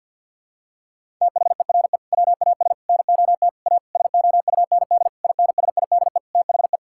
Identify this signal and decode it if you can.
Morse code, heard as a keyed tone, about 52 wpm, 700 Hz. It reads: THERE WAS NOT A SOUND INSIDE THE
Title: Sentences from Top 1000 Words - Review Part 6 - 50wpm